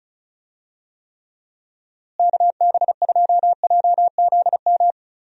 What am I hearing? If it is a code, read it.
Morse code, heard as a keyed tone, about 36 wpm, 700 Hz. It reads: KB2JZM
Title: USA Call Signs - 6 Characters - Repeated 3x - 35wpm